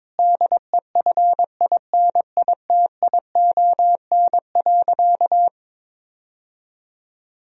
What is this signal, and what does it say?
Morse code, heard as a keyed tone, about 22 wpm, 700 Hz. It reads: DEFINITION.
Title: Sentences from Top 1000 Words - Review Part 4 - 22wpm